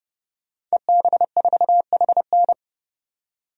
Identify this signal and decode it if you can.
Morse code, heard as a keyed tone, about 31 wpm, 700 Hz. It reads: EB4HN
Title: DXCC-100 Call Signs - 5 Characters - 30wpm